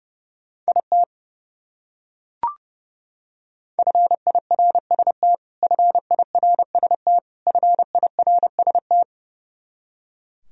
Morse code, heard as a keyed tone, about 30 wpm, 700 Hz. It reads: IT FIRST FIRST FIRST
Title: Top 100 Words Repeated 30wpm